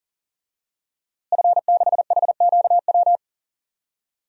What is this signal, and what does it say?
Morse code, heard as a keyed tone, about 40 wpm, 700 Hz. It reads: F6HQW
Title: DXCC-100 Call Signs - 5 Characters - Repeated 2x - 40wpm